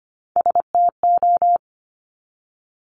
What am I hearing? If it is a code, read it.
Morse code, heard as a keyed tone, about 25 wpm, 700 Hz. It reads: STO
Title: Top Three Letter Combinations in Words 25wpm